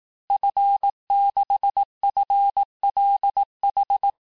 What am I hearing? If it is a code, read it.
Morse code, heard as a keyed tone, about 18 wpm, 800 Hz. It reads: F6FLH
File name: cours-du-lundi files / cours-1594--FIN-de-QSO-vitesse-17mpm